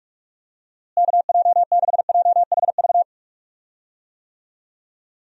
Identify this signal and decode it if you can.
Morse code, heard as a keyed tone, about 46 wpm, 700 Hz. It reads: KJ6JHV
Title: USA Call Signs - 6 Characters - Repeated 2x - 45wpm